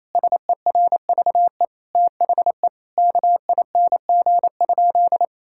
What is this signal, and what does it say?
Morse code, heard as a keyed tone, about 28 wpm, 700 Hz. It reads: SERVE THE KING?
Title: Sentences from Top 1600 Words 28wpm